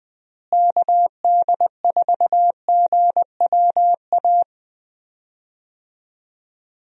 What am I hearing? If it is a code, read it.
Morse code, heard as a keyed tone, about 20 wpm, 700 Hz. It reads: KD4GWA
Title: USA Call Signs - 6 Characters - Repeated 2x - 20wpm